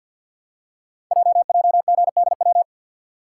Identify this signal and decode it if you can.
Morse code, heard as a keyed tone, about 51 wpm, 700 Hz. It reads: JJGDW
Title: Code Groups - 5 Letters - 50wpm